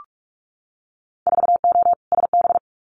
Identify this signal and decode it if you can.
Morse code, heard as a keyed tone, about 44 wpm, 700 Hz. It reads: 4O SB
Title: Winter Field Day Exchanges 45wpm